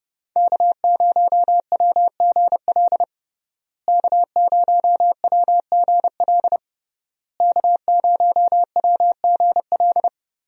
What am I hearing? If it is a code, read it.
Morse code, heard as a keyed tone, about 30 wpm, 700 Hz. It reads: K0WGL K0WGL K0WGL
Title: USA Call Signs - 5 Characters - Repeated 3x - 30wpm